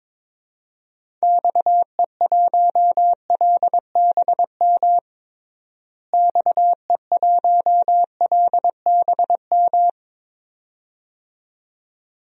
Morse code, heard as a keyed tone, about 22 wpm, 700 Hz. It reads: XE1LBM XE1LBM
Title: DXCC-100 Call Signs - 6 Characters - Repeated 2x - 22wpm